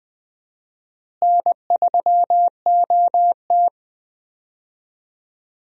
Morse code, heard as a keyed tone, about 20 wpm, 700 Hz. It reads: N3OT
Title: USA Call Signs - 4 Characters - Repeated 3x - 20wpm